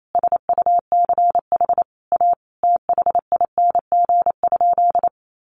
Morse code, heard as a keyed tone, about 28 wpm, 700 Hz. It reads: SUCH A THING?
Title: Sentences from Top 200 Words 28wpm